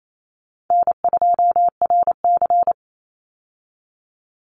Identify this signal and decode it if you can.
Morse code, heard as a keyed tone, about 28 wpm, 700 Hz. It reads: N2RC
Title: Easy Call Signs Repeated 2x - 28wpm